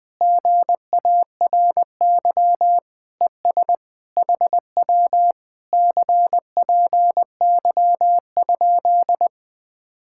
Morse code, heard as a keyed tone, about 20 wpm, 700 Hz. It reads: GARY ES HW CPY?
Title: Speed-Racing Common QSO Phrases - 20wpm